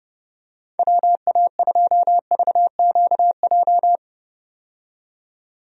Morse code, heard as a keyed tone, about 32 wpm, 700 Hz. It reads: WA2VQJ
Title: USA Call Signs - 6 Characters - 30wpm